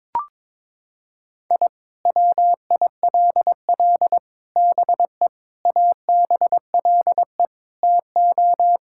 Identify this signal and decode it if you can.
Morse code, heard as a keyed tone, about 22 wpm, 700 Hz. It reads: I WILL BE ABLE TO
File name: practice / 22wpm / Sentences with 4-Word Phrases - 22wpm